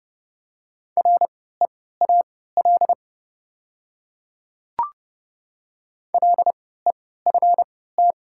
Morse code, heard as a keyed tone, about 30 wpm, spaced as for 17 wpm, 700 Hz. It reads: REAL LEFT
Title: 3x Character Spacing - Top 300 Words - 30wpm